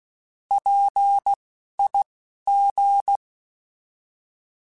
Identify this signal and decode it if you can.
Morse code, heard as a keyed tone, about 16 wpm, spaced as for 11 wpm, 800 Hz. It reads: PIG
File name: Alphabetical / Alphabetical mp3s / Ps @10